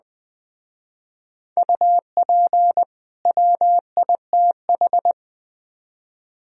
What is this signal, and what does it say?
Morse code, heard as a keyed tone, about 20 wpm, 700 Hz. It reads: UP WITH
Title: Speed-Racing - Sets of 2 Words 17wpm